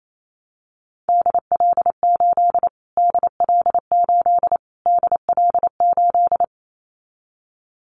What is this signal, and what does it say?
Morse code, heard as a keyed tone, about 28 wpm, 700 Hz. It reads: DL8 DL8 DL8
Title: Callsign Prefixes for Top 25 DXCC - Repeated 3x - 28wpm